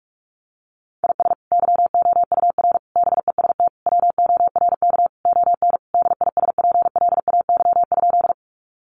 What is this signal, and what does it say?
Morse code, heard as a keyed tone, about 44 wpm, 700 Hz. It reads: IS YOUR BEST WORK ON DISPLAY?
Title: Sentences from Top 1200 Words 45wpm